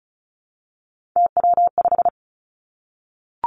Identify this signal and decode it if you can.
Morse code, heard as a keyed tone, about 36 wpm, 700 Hz. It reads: TW5
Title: Three Challenging Letter-Number Combinations 35wpm